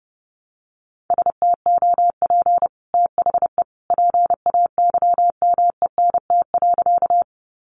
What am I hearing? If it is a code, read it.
Morse code, heard as a keyed tone, about 30 wpm, 700 Hz. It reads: STOP THE PAYMENT.
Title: Sentences from Top 1200 Words 30wpm